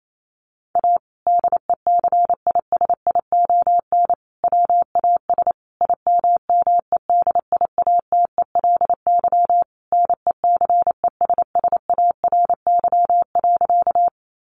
Morse code, heard as a keyed tone, about 28 wpm, 700 Hz. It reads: A DECISION WAS IMMEDIATELY NECESSARY.
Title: Sentences from Top 1100 Words 28wpm